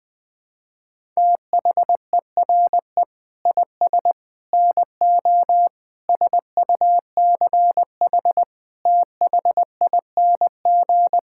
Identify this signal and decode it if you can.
Morse code, heard as a keyed tone, about 20 wpm, 700 Hz. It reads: THERE IS NO SUCH THING
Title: Sentences with 9-Word Phrases - Spoken Beginning - 20wpm